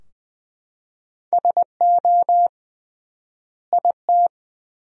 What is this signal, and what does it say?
Morse code, heard as a keyed tone, about 21 wpm, 700 Hz. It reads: SO IT
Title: 3x Word-Spacing - Sentences from Top 300 Words - 20wpm